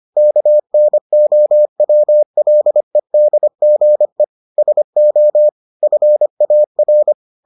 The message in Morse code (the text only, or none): KNOWLEDGE SO FAR